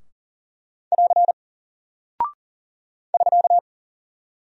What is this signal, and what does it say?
Morse code, heard as a keyed tone, about 40 wpm, 700 Hz. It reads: <AR> <SK>
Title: Single Character (Letter:Number:Punctuation:Prosign) Alt 40wpm